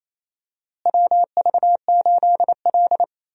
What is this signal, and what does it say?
Morse code, heard as a keyed tone, about 28 wpm, 700 Hz. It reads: WV8L
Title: USA Call Signs - 4 Characters - Repeated 2x - 28wpm